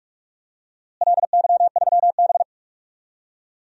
Morse code, heard as a keyed tone, about 46 wpm, 700 Hz. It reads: LY3B
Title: DXCC-100 Call Signs - 4 Characters - Repeated 3x - 45wpm